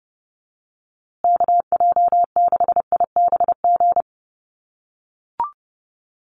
Morse code, heard as a keyed tone, about 30 wpm, 700 Hz. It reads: KJ6IBG
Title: USA Call Signs Repeated 2x 30wpm